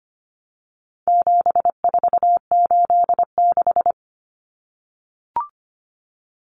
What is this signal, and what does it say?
Morse code, heard as a keyed tone, about 25 wpm, 700 Hz. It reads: 7486
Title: Common Cut Numbers - 4-Digit 25wpm